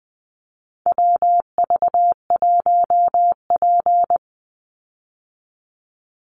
Morse code, heard as a keyed tone, about 20 wpm, 700 Hz. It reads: WV1P
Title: USA Call Signs 20wpm